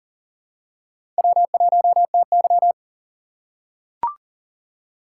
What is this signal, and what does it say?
Morse code, heard as a keyed tone, about 38 wpm, 700 Hz. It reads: W1TY
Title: USA Call Signs - 4 Characters - 3x Character Spacing - 40wpm